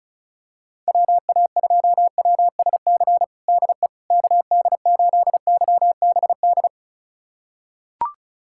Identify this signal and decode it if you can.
Morse code, heard as a keyed tone, about 34 wpm, 700 Hz. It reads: WA2WSC DE KD8YBD
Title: USA Callsign DE Callsign - 6 Characters - 35wpm